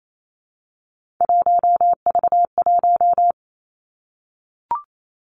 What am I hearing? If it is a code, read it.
Morse code, heard as a keyed tone, about 28 wpm, 700 Hz. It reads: 1V1
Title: Three Challenging Letter-Number Combinations 28wpm